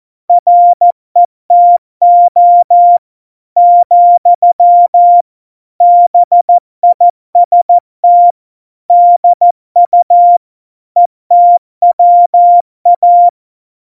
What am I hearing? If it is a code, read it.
Morse code, heard as a keyed tone, about 14 wpm, 700 Hz. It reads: RETO , BIST DU ETWA
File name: Morsekurs / Witze / MP3, Witz 10 April 2018, 70 F80